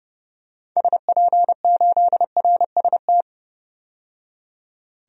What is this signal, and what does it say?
Morse code, heard as a keyed tone, about 30 wpm, 700 Hz. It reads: SP8RST
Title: DXCC-100 Call Signs - 6 Characters - 30wpm